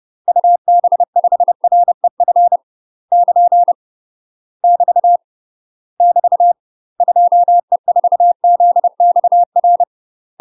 Morse code, heard as a keyed tone, about 30 wpm, 700 Hz. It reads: UB5REF <KN> <BT> <BT> 2E4ZXR
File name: UK MASTER QSOs TRAINING / 30WPM / UK MASTER QSOs 0027